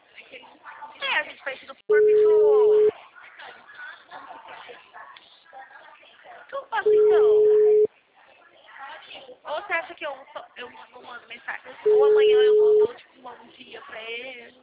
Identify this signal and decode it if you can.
Ringback tone (Europe, 425 Hz), with speech.